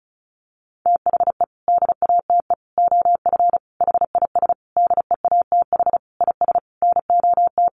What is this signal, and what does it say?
Morse code, heard as a keyed tone, about 34 wpm, 700 Hz. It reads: THE DATE OF HIS DEATH IS NOT
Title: Sentences from Top 1100 Words 35wpm